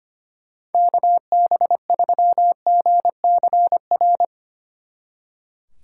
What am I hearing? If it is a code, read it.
Morse code, heard as a keyed tone, about 32 wpm, 700 Hz. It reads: KB3GCR